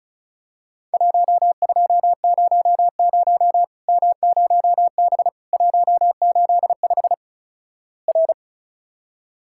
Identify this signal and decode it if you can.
Morse code, heard as a keyed tone, about 36 wpm, 700 Hz. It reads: 1200 M0B 185 R
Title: Worked All Europe DX Contest - QTC Exchange - 35wpm